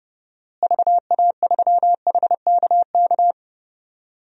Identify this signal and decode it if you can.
Morse code, heard as a keyed tone, about 31 wpm, 700 Hz. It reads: VA3HKK
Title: DXCC-100 Call Signs - 6 Characters - Repeated 2x - 30wpm